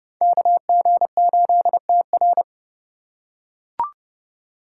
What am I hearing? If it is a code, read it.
Morse code, heard as a keyed tone, about 29 wpm, 700 Hz. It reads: KG8TR